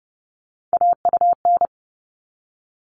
Morse code, heard as a keyed tone, about 30 wpm, 700 Hz. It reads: AUN
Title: All Cut Numbers - 3-Digit 30wpm